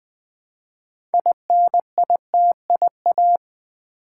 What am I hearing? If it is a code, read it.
Morse code, heard as a keyed tone, about 20 wpm, 700 Hz. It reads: INITIA
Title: Build A Word - 3 Letters At A Time - 9-Letter Words - 20wpm